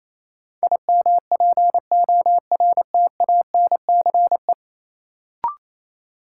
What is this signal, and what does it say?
Morse code, heard as a keyed tone, about 28 wpm, 700 Hz. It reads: IMPORTANCE
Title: Sentences from Top 1200 Words - New Word Drill Alt - 28wpm